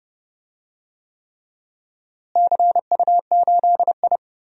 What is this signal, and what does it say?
Morse code, heard as a keyed tone, about 30 wpm, 700 Hz. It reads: CU8I
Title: DXCC-100 Call Signs - 4 Characters - Repeated 3x - 30wpm